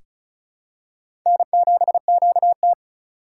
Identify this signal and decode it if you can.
Morse code, heard as a keyed tone, about 36 wpm, 700 Hz. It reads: N7QT
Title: K1USN SST QSO Practice - 35wpm